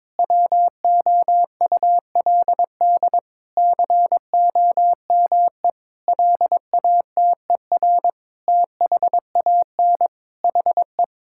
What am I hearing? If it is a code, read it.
Morse code, heard as a keyed tone, about 22 wpm, 700 Hz. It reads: WOULD COME LATER THAN HE
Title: Sentences from Top 1600 Words 22wpm